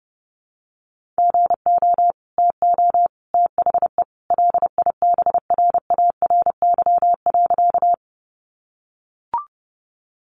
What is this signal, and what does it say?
Morse code, heard as a keyed tone, about 30 wpm, 700 Hz. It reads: GO TO THE LIBRARY.
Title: Sentences from Top 600 Words 30wpm